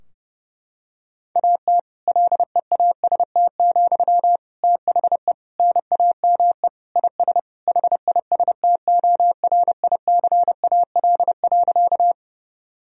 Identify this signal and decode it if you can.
Morse code, heard as a keyed tone, about 30 wpm, 700 Hz. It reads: AT LEAST, THE NAME IS HISTORICAL.
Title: Sentences from Top 1000 Words - Review Part 4 - 30wpm